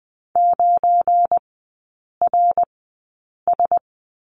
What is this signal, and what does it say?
Morse code, heard as a keyed tone, about 20 wpm, spaced as for 8 wpm, 690 Hz. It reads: 9RS
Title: 172 Lesson 28e - Number 9 - Three Characters 20wpm